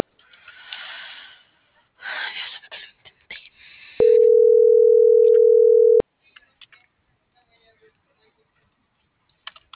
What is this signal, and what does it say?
Ringback tone (North America), 440 + 480 Hz.